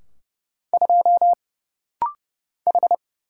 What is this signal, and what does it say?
Morse code, heard as a keyed tone, about 30 wpm, 700 Hz. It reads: 2 H
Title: Single Letter-Number Alt - Rapid-Fire 30wpm